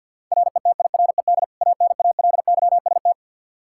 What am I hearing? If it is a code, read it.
Morse code, heard as a keyed tone, about 51 wpm, 700 Hz. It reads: RETIRED ANALYST
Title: Practice QSO - Exchange 3 - More Info - 50wpm